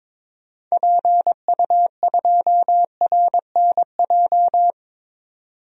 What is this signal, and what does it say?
Morse code, heard as a keyed tone, about 22 wpm, 700 Hz. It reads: PU2RNJ
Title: DXCC-100 Call Signs - 6 Characters - 22wpm